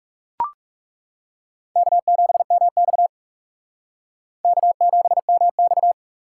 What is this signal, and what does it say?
Morse code, heard as a keyed tone, about 37 wpm, 700 Hz. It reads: K7MX K7MX
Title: Speed Racing - USA 1x2 Call Signs 30wpm